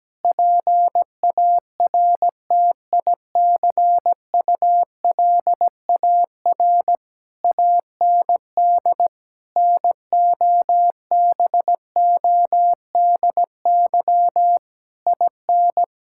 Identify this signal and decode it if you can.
Morse code, heard as a keyed tone, about 17 wpm, 700 Hz. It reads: PARTICULAR AND NOBODY IN
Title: Sentences from Top 1600 Words 17wpm